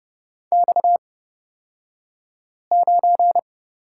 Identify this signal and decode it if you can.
Morse code, heard as a keyed tone, about 31 wpm, spaced as for 5 wpm, 700 Hz. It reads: X9
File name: Lesson 38d - Letter q - Two Characters 30wpm